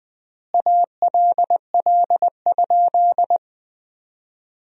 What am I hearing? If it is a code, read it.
Morse code, heard as a keyed tone, about 20 wpm, 700 Hz. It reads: ALL?
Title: 4x Word-Spacing - Sentences from Top 200 Words - 20wpm